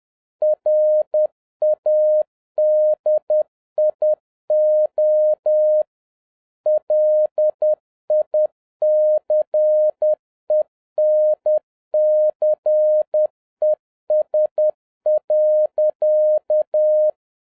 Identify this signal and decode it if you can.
Morse code, heard as a keyed tone, about 10 wpm, 610 Hz. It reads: RADIO LICENCES.